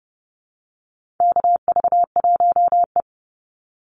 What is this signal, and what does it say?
Morse code, heard as a keyed tone, about 30 wpm, 700 Hz. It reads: KV1E
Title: Call Signs Repeated 2x - 30wpm